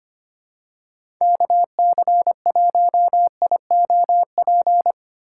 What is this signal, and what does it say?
Morse code, heard as a keyed tone, about 25 wpm, 700 Hz. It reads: KC1IOP
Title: USA Call Signs - 6 Characters - 6x Character Spacing - 25wpm